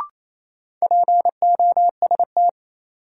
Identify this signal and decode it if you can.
Morse code, heard as a keyed tone, about 28 wpm, 700 Hz. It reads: POST
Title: Sentences from Top 1000 Words - New Word Drill - 28wpm